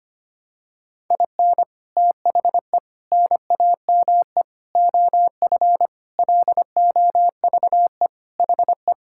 Morse code, heard as a keyed tone, about 25 wpm, 700 Hz. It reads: IN THE NAME OF LOVE HE
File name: practice / 25wpm / Sentences with 4-Word Phrases - 25wpm